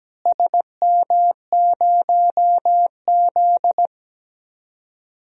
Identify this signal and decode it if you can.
Morse code, heard as a keyed tone, about 17 wpm, 700 Hz. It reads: SM0Z
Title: DXCC-100 Call Signs - 4 Characters - Repeated 3x - 17wpm